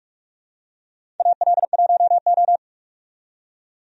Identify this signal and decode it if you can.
Morse code, heard as a keyed tone, about 44 wpm, 700 Hz. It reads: AL1O
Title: USA Call Signs - 4 Characters - 45wpm